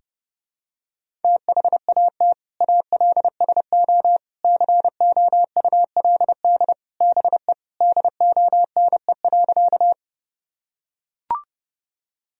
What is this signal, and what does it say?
Morse code, heard as a keyed tone, about 30 wpm, 700 Hz. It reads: THAT ALSO COULD BE DONE.
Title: Sentences from Top 1000 Words - Review Part 1 - 30wpm